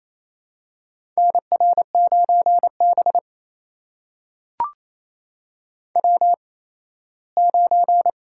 Morse code, heard as a keyed tone, about 28 wpm, spaced as for 17 wpm, 700 Hz. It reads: NR9B W9
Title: USA Call Signs - 4 Characters - 8x Character Spacing - 28wpm